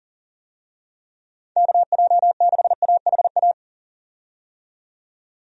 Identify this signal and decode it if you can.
Morse code, heard as a keyed tone, about 40 wpm, 700 Hz. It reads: KJ6AHA
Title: USA Call Signs - 6 Characters - 40wpm